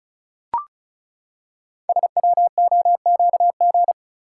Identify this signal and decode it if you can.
Morse code, heard as a keyed tone, about 36 wpm, 700 Hz. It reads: SWOQG
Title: Code Groups - 5 Letters - 35wpm